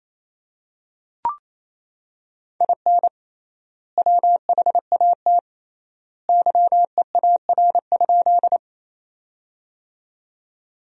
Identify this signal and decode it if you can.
Morse code, heard as a keyed tone, about 28 wpm, 700 Hz. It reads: IN WHAT YEAR?
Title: 3x Word-Spacing - Sentences from Top 200 Words - 28wpm